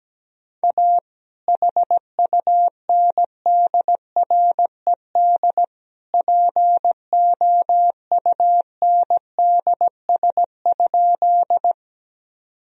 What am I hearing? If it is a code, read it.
Morse code, heard as a keyed tone, about 17 wpm, 700 Hz. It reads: A HUNDRED POUNDS?
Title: Sentences from Top 1000 Words - Review Part 2 - 17wpm